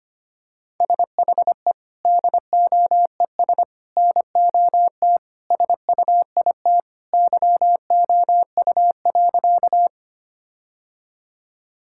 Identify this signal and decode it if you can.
Morse code, heard as a keyed tone, about 25 wpm, 700 Hz. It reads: SHE DOES NOT SUIT YOU.
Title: Sentences from Top 1600 Words 25wpm